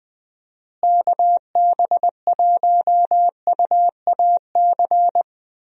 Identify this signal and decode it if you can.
Morse code, heard as a keyed tone, about 20 wpm, 700 Hz. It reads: KB1UAC
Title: USA Call Signs - 6 Characters - Repeated 3x - 20wpm